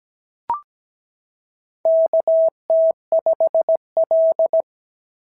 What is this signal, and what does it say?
Morse code, heard as a keyed tone, about 17 wpm, 690 Hz. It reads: KT5L
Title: SOTA QSO Practice 17wpm